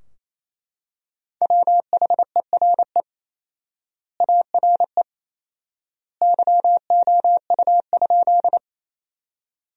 Morse code, heard as a keyed tone, about 28 wpm, 700 Hz. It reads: WHERE ARE YOU?
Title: 4x Word-Spacing - Sentences from Top 200 Words - 28wpm